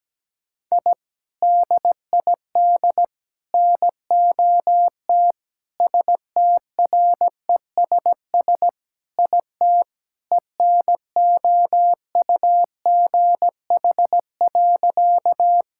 Morse code, heard as a keyed tone, about 17 wpm, 700 Hz. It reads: I DID NOT STRESS IT ENOUGH.
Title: Sentences from Top 1000 Words - Review Part 6 - 17wpm